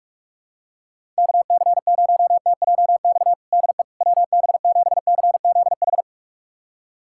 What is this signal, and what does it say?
Morse code, heard as a keyed tone, about 41 wpm, 700 Hz. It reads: KC0TJX DE WB7CZH
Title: USA Callsign DE Callsign 45wpm